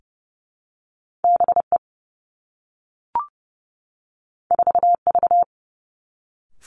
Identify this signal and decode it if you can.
Morse code, heard as a keyed tone, about 30 wpm, 700 Hz. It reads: B E 4 V
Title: Two Challenging Letter-Number Combinations 30wpm